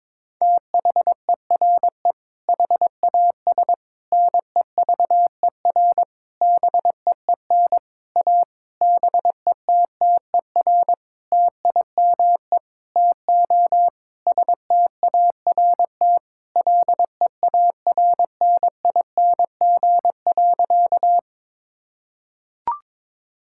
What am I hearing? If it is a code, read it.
Morse code, heard as a keyed tone, about 22 wpm, 700 Hz. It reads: THERE HAS NEVER BEEN A BETTER TIME TO START LEARNING.
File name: Sentences with 7-Word Phrases - 22wpm